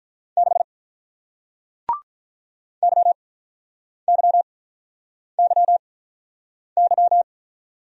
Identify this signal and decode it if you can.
Morse code, heard as a keyed tone, about 48 wpm, spaced as for 9 wpm, 700 Hz. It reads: 6 YYYY